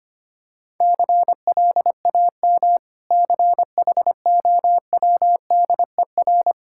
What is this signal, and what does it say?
Morse code, heard as a keyed tone, about 25 wpm, 690 Hz. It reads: CLAM CHOWDER
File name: practice / 25wpm / Difficult Words with Hints - 25wpm